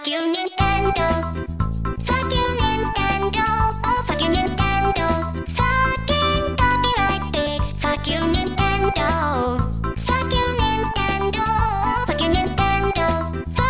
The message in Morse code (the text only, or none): none